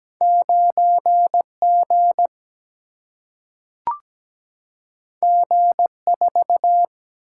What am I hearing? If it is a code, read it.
Morse code, heard as a keyed tone, about 17 wpm, 700 Hz. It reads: 9 G G 4
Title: Single-Letter Single-Number 17wpm